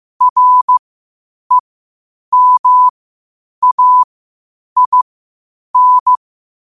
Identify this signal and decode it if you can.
Morse code, heard as a keyed tone, about 15 wpm, spaced as for 8 wpm, 1000 Hz. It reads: REMAIN
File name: qotdfiles / 8wpm